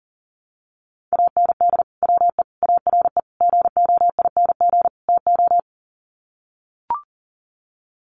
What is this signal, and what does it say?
Morse code, heard as a keyed tone, about 40 wpm, 700 Hz. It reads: AND WE ARE GOING TO
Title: Sets of 5 Words Encore2 40wpm